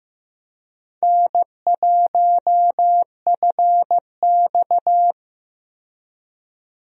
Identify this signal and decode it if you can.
Morse code, heard as a keyed tone, about 15 wpm, 700 Hz. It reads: N1FX
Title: USA Call Signs - 4 Characters - 15wpm